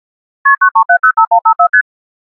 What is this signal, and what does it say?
Multi-frequency trunk signalling, heard as fast pairs of tones, tones of about 100 ms. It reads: KP93705154ST